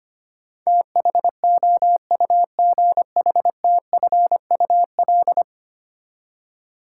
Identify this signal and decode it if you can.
Morse code, heard as a keyed tone, about 25 wpm, 700 Hz. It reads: THOUGHTFUL
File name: Ten Letter Words - 25wpm